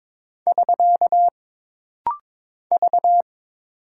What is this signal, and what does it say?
Morse code, heard as a keyed tone, about 23 wpm, 700 Hz. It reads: <SK> V
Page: This rapid-fire timing is geared to push your ICR (Instant Character Recognition) skills.